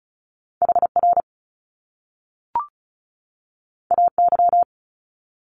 Morse code, heard as a keyed tone, about 36 wpm, 700 Hz. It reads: HR AY
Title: Top Two Letter Combinations in Words 35wpm